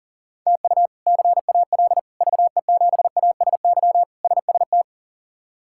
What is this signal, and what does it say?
Morse code, heard as a keyed tone, about 40 wpm, 700 Hz. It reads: TU CAL VE7ASY SST
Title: K1USN SST QSO Practice - 40wpm